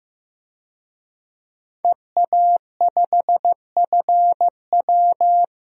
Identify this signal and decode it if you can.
Morse code, heard as a keyed tone, about 16 wpm, 700 Hz. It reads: EA5FW
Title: DXCC-100 Call Signs - 5 Characters - 15wpm